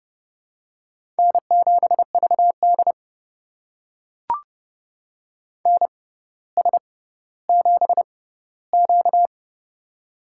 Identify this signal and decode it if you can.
Morse code, heard as a keyed tone, about 30 wpm, spaced as for 15 wpm, 700 Hz. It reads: N7VD NS7Q
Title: USA Call Signs - 4 Characters - 6x Character Spacing - 30wpm